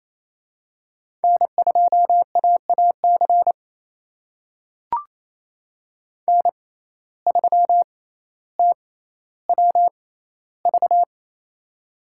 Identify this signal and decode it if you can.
Morse code, heard as a keyed tone, about 28 wpm, spaced as for 14 wpm, 700 Hz. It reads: N2AAC N3TWV
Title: USA Call Signs - 5 Characters - 6x Character Spacing - 28wpm